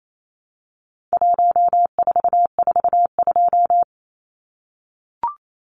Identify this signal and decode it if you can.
Morse code, heard as a keyed tone, about 28 wpm, 700 Hz. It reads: 1442
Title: Numbers - 4-Digit 28wpm